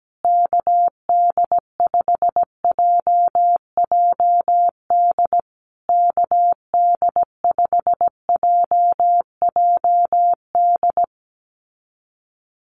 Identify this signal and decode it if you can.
Morse code, heard as a keyed tone, about 17 wpm, 700 Hz. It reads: KD5JJD KD5JJD
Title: USA Call Signs Repeated 3x 17wpm